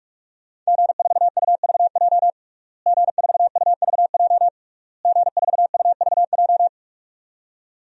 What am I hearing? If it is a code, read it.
Morse code, heard as a keyed tone, about 46 wpm, 700 Hz. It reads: G4UVJ G4UVJ G4UVJ
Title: DXCC-100 Call Signs - 5 Characters - Repeated 3x - 45wpm